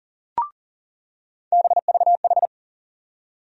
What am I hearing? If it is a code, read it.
Morse code, heard as a keyed tone, about 40 wpm, 700 Hz. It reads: BVH
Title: USA Call Sign Postfixes - 3 Letters 40wpm